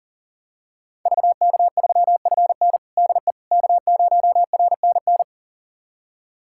Morse code, heard as a keyed tone, about 40 wpm, 700 Hz. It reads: VK3FN DE K0RNN
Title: USA Callsign DE Callsign - 5 Characters - 40wpm